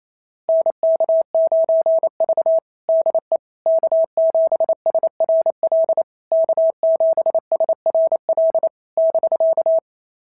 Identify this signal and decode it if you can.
Morse code, heard as a keyed tone, about 25 wpm, 650 Hz. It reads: NK9V DE K7SRL K7SRL <BK>
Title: SOTA QSO Practice 28wpm